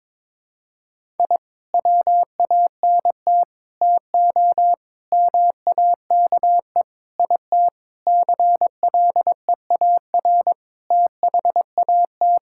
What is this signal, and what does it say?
Morse code, heard as a keyed tone, about 22 wpm, 700 Hz. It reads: I WANT TO MAKE IT CLEAR THAT
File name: Sentences with 7-Word Phrases - 22wpm